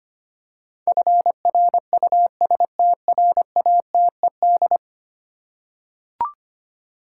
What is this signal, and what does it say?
Morse code, heard as a keyed tone, about 25 wpm, 700 Hz. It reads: FRUSTRATED
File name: 10 Letter Words - First 3 Letters Given - 25wpm